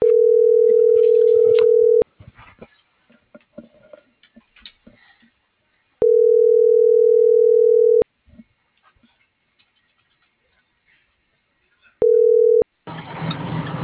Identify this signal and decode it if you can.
Ringback tone (North America), 440 + 480 Hz.